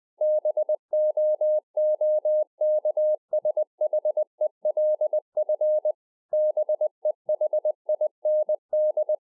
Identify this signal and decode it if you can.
Morse code, heard as a keyed tone, about 20 wpm, 610 Hz. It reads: BOOKSHELF BEHIND